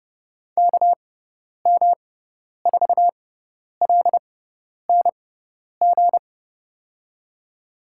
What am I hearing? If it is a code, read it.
Morse code, heard as a keyed tone, about 30 wpm, spaced as for 10 wpm, 700 Hz. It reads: KM4LNG
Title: USA Call Signs - 6x Character Spacing - 30wpm